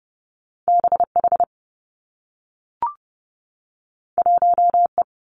Morse code, heard as a keyed tone, about 30 wpm, 700 Hz. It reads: B H 1 E
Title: Two Challenging Letter-Number Combinations 30wpm